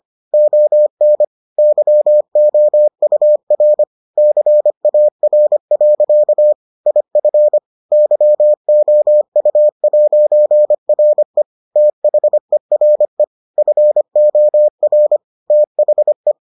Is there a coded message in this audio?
Yes: Morse code from 0.3 to 16.3 s, heard as a keyed tone, about 25 wpm, 600 Hz. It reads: ON YOUR CAR. IF YOU'RE THERE FOR THE